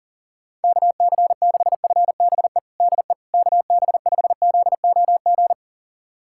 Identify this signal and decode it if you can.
Morse code, heard as a keyed tone, about 40 wpm, 700 Hz. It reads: KC6FBE DE KB5ZOG
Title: USA Callsign DE Callsign - 6 Characters - 40wpm